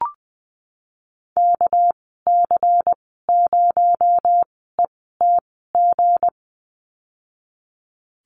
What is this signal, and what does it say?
Morse code, heard as a keyed tone, about 20 wpm, spaced as for 15 wpm, 700 Hz. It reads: KC0ETG
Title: 219 Lesson 35g - Number 0 - Call Signs 20wpm